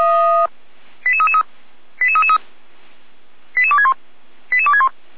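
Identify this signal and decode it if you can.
Five-tone selective call (ZVEI), heard as quick runs of five tones, tones of about 80 ms. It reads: 88333 88333 88261 88261
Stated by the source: ZVEI five-tone call received by SDR, recorded by Feuerwehr Orlamünde.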